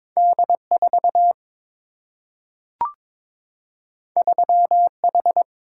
Morse code, heard as a keyed tone, about 22 wpm, 700 Hz. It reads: D 4 3 H
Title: Single-Letter Single-Number 22wpm